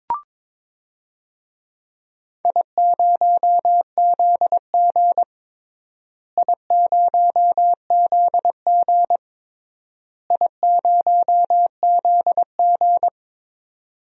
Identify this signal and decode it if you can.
Morse code, heard as a keyed tone, about 22 wpm, 700 Hz. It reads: I0ZG I0ZG I0ZG
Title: DXCC-100 Call Signs - 4 Characters - Repeated 3x - 22wpm